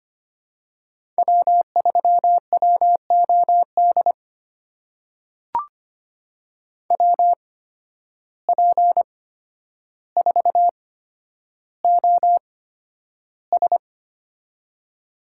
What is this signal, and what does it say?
Morse code, heard as a keyed tone, about 25 wpm, spaced as for 11 wpm, 700 Hz. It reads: W3WOD WP4OS
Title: USA Call Signs - 5 Characters - 8x Character Spacing - 25wpm